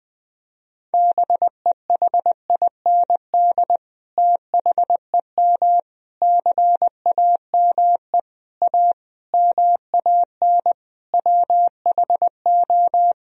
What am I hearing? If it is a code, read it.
Morse code, heard as a keyed tone, about 20 wpm, 700 Hz. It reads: BEHIND THEM CAME A MAN WHO
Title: Sentences from Top 1000 Words - Review Part 4 - 20wpm